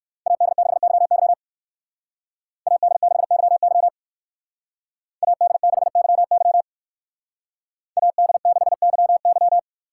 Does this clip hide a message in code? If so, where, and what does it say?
Yes: Morse code from 0.3 to 9.6 s, heard as a keyed tone, about 58 wpm, 700 Hz. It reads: AD6YY AD6YY AD6YY AD6YY